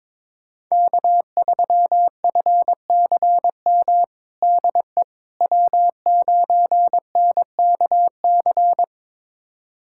Morse code, heard as a keyed tone, about 22 wpm, 700 Hz. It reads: K3FCM DE W9NKC
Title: USA Callsign DE Callsign - 5 Characters - 22wpm